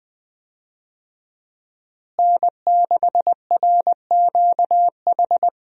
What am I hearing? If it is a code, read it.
Morse code, heard as a keyed tone, about 20 wpm, 700 Hz. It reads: N6RQH
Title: USA Call Signs - 5 Characters - Repeated 3x - 20wpm